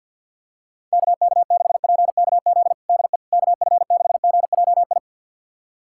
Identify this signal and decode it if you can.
Morse code, heard as a keyed tone, about 50 wpm, 700 Hz. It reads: KK6PKZ DE KF6GJI
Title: USA Callsign DE Callsign 50wpm